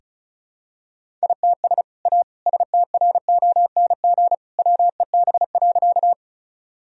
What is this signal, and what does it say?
Morse code, heard as a keyed tone, about 34 wpm, 700 Hz. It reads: ITS A STRONG WEB.
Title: Sentences from Top 1000 Words - Review Part 4 - 35wpm